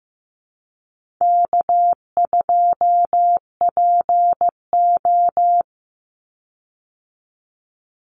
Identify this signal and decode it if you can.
Morse code, heard as a keyed tone, about 15 wpm, 700 Hz. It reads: K2PO
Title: Easy Call Signs Repeated 2x - 15wpm